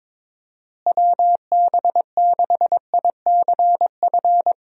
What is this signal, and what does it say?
Morse code, heard as a keyed tone, about 21 wpm, 700 Hz. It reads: WB6ICF
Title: USA Call Signs - 6 Characters - Repeated 2x - 22wpm